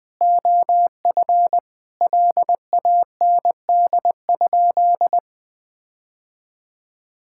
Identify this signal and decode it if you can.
Morse code, heard as a keyed tone, about 21 wpm, 700 Hz. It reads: OF LAND?
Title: Sentences from Top 1600 Words 20wpm